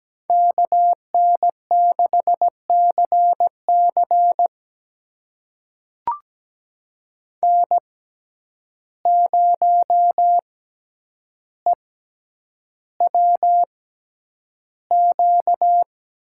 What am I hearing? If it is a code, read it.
Morse code, heard as a keyed tone, about 17 wpm, spaced as for 9 wpm, 700 Hz. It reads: KN6CC N0EWQ
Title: USA Call Signs - 5 Characters - 6x Character Spacing - 17wpm